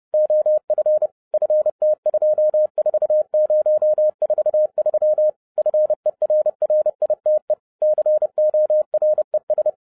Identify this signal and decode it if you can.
Morse code, heard as a keyed tone, about 30 wpm, 610 Hz. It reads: OF FT24043 FERRITE CORES
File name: 2012-01-mp3 / n12-I DIDNT MISS THE SOLAR CYCLE 24-30wpm